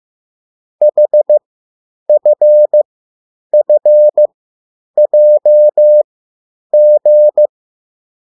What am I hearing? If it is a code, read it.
Morse code, heard as a keyed tone, about 15 wpm, spaced as for 9 wpm, 600 Hz. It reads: HFFJG
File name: urok 1